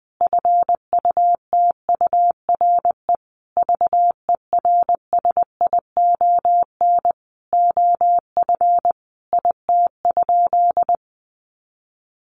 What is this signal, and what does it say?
Morse code, heard as a keyed tone, about 20 wpm, 700 Hz. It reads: FUTURE VERSION OF IT?
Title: Sentences from Top 1400 Words 20wpm